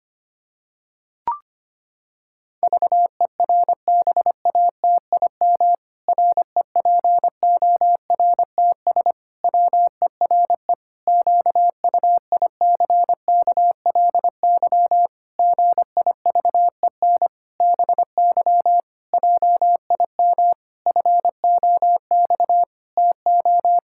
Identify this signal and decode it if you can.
Morse code, heard as a keyed tone, about 25 wpm, 700 Hz. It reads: VERBATIM REPORTS WERE QUICKLY GIVEN BY JIM FOX TO
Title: Pangram Challenge v2 - 25wpm